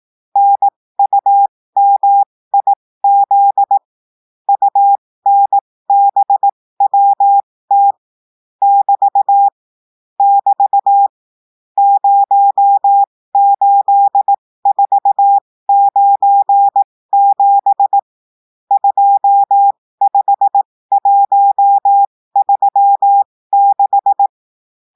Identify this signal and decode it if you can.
Morse code, heard as a keyed tone, about 18 wpm, spaced as for 15 wpm, 800 Hz. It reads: NUMIZ UNBWT <BT> <BT> 08497 25136
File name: cours-du-lundi files / cours-1580-texte-chiffre-15-mpm